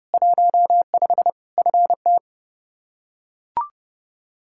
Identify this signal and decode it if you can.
Morse code, heard as a keyed tone, about 32 wpm, 700 Hz. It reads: 15 FT
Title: Practice QSO - Exchange 2 - Info - 30wpm